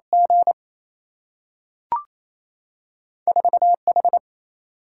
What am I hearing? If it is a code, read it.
Morse code, heard as a keyed tone, about 28 wpm, 700 Hz. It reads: G 4 H
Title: Single-Letter Single-Number 28wpm